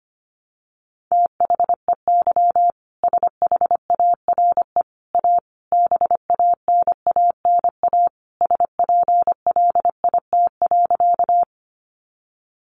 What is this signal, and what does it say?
Morse code, heard as a keyed tone, about 25 wpm, 700 Hz. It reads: THEY SHARE A BANANA SPLIT.
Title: Additional Easy Phrases 25wpm